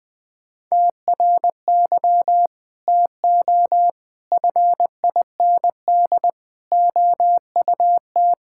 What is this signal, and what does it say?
Morse code, heard as a keyed tone, about 21 wpm, 700 Hz. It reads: TRY TO FIND OUT
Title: Sentences with 3-Word Phrases - 20wpm